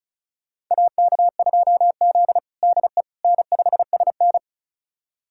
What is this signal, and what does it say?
Morse code, heard as a keyed tone, about 36 wpm, 700 Hz. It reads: AK2Z DE N5SN
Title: USA Callsign DE Callsign - 4 Characters - 35wpm